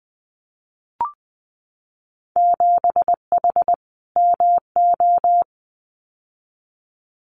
Morse code, heard as a keyed tone, about 21 wpm, 700 Hz. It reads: 7H MO
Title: Winter Field Day Exchanges Encore 20wpm